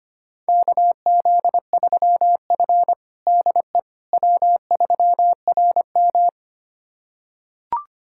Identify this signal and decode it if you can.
Morse code, heard as a keyed tone, about 25 wpm, 700 Hz. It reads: KZ3F DE W3RM
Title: USA Callsign DE Callsign - 4 Characters - 25wpm